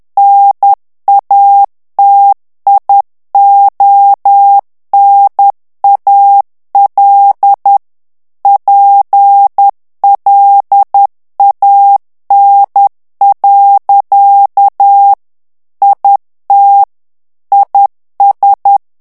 Morse code, heard as a keyed tone, about 10 wpm, 800 Hz. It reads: NATIONAL PLAN. IT IS
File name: morse mp3 / ANGLAIS texte clair 2 vitesse 10 mots minute